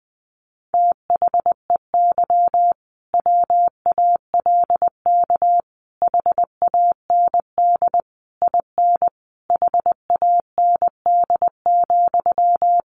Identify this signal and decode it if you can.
Morse code, heard as a keyed tone, about 20 wpm, 700 Hz. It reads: THEY WALK HAND IN HAND,
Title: Sentences from Top 1400 Words 20wpm